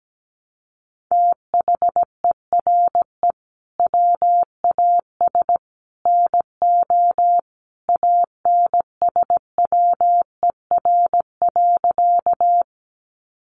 Morse code, heard as a keyed tone, about 17 wpm, 700 Hz. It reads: THERE WAS NO ANSWER.
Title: Sentences from Top 300 Words 17wpm